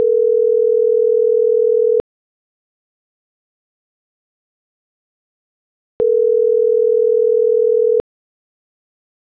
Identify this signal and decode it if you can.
Ringback tone (North America), 440 + 480 Hz.
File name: CENTRAL NMP / out jacqueline.castronmp